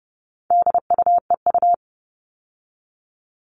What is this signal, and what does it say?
Morse code, heard as a keyed tone, about 30 wpm, 700 Hz. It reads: DUEU
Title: All Cut Numbers - 4-Digit 30wpm